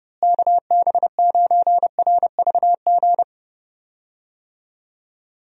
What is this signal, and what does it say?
Morse code, heard as a keyed tone, about 30 wpm, 700 Hz. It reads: KB9RVG
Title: USA Call Signs - 6 Characters - Repeated 2x - 30wpm